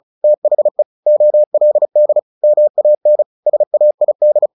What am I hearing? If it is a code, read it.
Morse code, heard as a keyed tone, about 34 wpm, 600 Hz. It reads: THE OLD MAN SAID
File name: the old man and the sea 2 35wpm